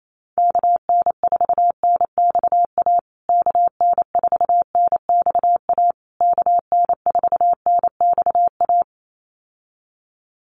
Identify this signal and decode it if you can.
Morse code, heard as a keyed tone, about 28 wpm, 700 Hz. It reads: KN4NXA KN4NXA KN4NXA
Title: USA Call Signs Repeated 3x 28wpm